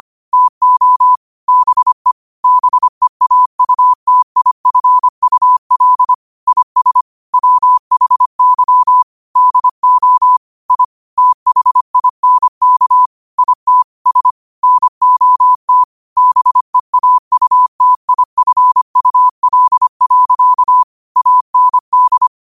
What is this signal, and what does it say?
Morse code, heard as a keyed tone, about 25 wpm, 1000 Hz. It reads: TO BE BEAUTIFUL IS WHY DO I THINK IT S NOT BEAUTIFUL. AND
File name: qotdfiles / 25wpm